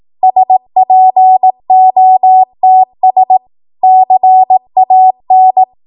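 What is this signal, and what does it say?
Morse code, heard as a keyed tone, about 18 wpm, 750 Hz. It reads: SPOTS CAN
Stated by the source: Code practice at 18 WPM, a transition run bridging 15 to 20 WPM.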